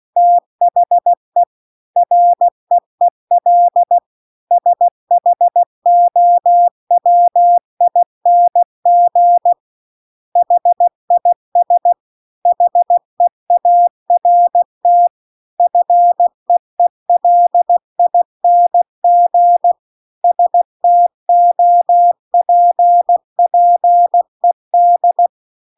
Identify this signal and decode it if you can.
Morse code, heard as a keyed tone, about 16 wpm, 700 Hz. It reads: THE REEL SHOWING HIS HEART FEELING STOPPED
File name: CW-The-Big-Twoheaded-River-2-16WPM